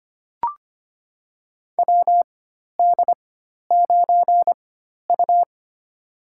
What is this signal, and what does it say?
Morse code, heard as a keyed tone, about 25 wpm, spaced as for 12 wpm, 700 Hz. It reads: WD9U
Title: USA Call Signs - 4 Characters - 4x Character Spacing - 25wpm